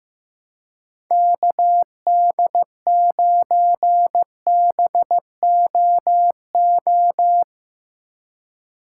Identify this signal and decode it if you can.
Morse code, heard as a keyed tone, about 15 wpm, 700 Hz. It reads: KD9BOO
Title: USA Call Signs - 3x Character Spacing - 15wpm